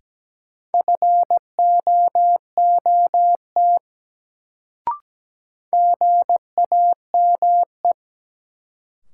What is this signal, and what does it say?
Morse code, heard as a keyed tone, about 17 wpm, 700 Hz. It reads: FOOT GAME
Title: Sentences from Top 500 Words - New Word Drill - 17wpm